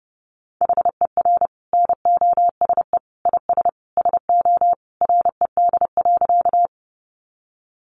Morse code, heard as a keyed tone, about 30 wpm, 700 Hz. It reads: HER NOSE IS SO RED.
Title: Easy Phrases 30wpm